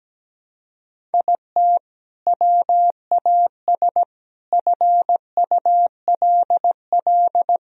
Morse code, heard as a keyed tone, about 17 wpm, 700 Hz. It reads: IT WAS FULL
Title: Sentences from Top 1000 Words - Review Part 3 - 17wpm